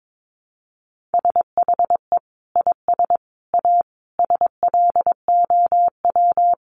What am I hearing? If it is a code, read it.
Morse code, heard as a keyed tone, about 22 wpm, 700 Hz. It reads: SHE IS A SLOW
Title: Additional Easy Phrases 22wpm